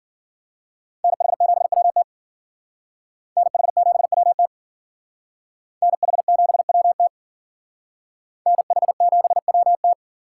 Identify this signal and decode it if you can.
Morse code, heard as a keyed tone, about 53 wpm, 700 Hz. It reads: NH7WT NH7WT NH7WT NH7WT